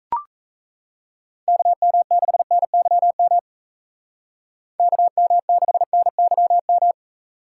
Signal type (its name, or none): Morse code